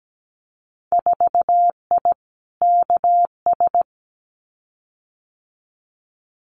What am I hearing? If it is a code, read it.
Morse code, heard as a keyed tone, about 17 wpm, 700 Hz. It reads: 4I KS
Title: Winter Field Day Exchanges Encore 17wpm